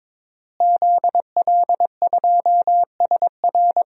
Morse code, heard as a keyed tone, about 22 wpm, 700 Hz. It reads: ZL2SR
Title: DXCC-100 Call Signs - 5 Characters - Repeated 3x - 22wpm